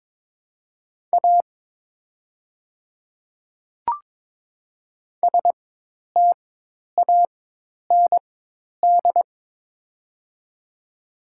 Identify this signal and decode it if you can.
Morse code, heard as a keyed tone, about 22 wpm, spaced as for 10 wpm, 700 Hz. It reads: A STAND
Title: 4x Character Spacing - Top 300 Words - 22wpm